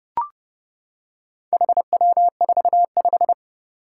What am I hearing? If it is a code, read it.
Morse code, heard as a keyed tone, about 30 wpm, 700 Hz. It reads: HW45
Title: Four Challenging Letter-Number Combinations 30wpm